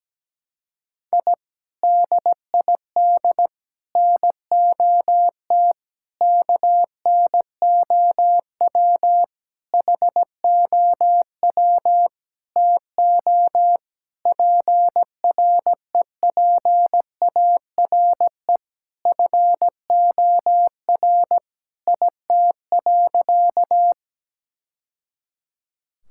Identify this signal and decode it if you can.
Morse code, heard as a keyed tone, about 17 wpm, 700 Hz. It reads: I DID NOT KNOW HOW TO PREPARE FOR IT.
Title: Sentences from Top 1000 Words - Review Part 6 - 17wpm